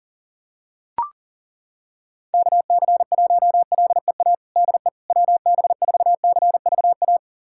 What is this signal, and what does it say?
Morse code, heard as a keyed tone, about 40 wpm, 700 Hz. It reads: KC1LEA DE WB4CVA
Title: USA Callsign DE Callsign 40wpm